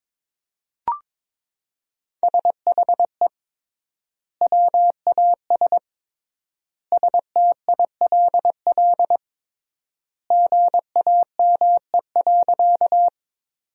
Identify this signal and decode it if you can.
Morse code, heard as a keyed tone, about 21 wpm, 700 Hz. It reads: SHE WAS STILL GAME.
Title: 3x Word-Spacing - Sentences from Top 500 Words - 22wpm